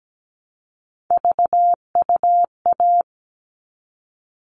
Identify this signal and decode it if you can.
Morse code, heard as a keyed tone, about 17 wpm, 700 Hz. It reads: VUA
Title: All Cut Numbers - 3-Digit 17wpm